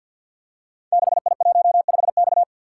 Morse code, heard as a keyed tone, about 51 wpm, 700 Hz. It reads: 6I15X
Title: Code Groups - 5 Letter:Numbers - 50wpm